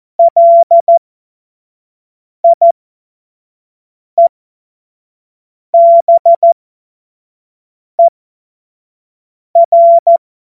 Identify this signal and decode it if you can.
Morse code, heard as a keyed tone, about 14 wpm, spaced as for 5 wpm, 680 Hz. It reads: LIEBER
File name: Morsekurs / Witze / Witz 11. Januar 2022, 25 F70